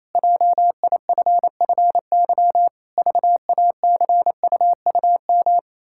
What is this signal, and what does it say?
Morse code, heard as a keyed tone, about 28 wpm, 700 Hz. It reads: JIFFY VACUUM
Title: Pangram Challenge v2 - 28wpm